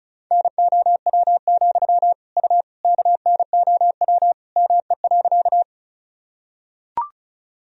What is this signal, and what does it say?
Morse code, heard as a keyed tone, about 36 wpm, 700 Hz. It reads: NOW, U KNOW ME.
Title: Sentences from Top 200 Words and Abbreviations 35wpm